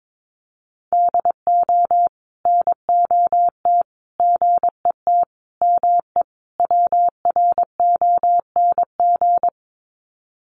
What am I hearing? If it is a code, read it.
Morse code, heard as a keyed tone, about 22 wpm, 700 Hz. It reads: DO NOT GET ME WRONG
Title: Sets of 5 Words Encore 22wpm